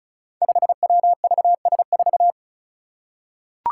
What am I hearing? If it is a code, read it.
Morse code, heard as a keyed tone, about 34 wpm, 700 Hz. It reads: 5WVS4
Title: Five Challenging Letter-Number Combinations 35wpm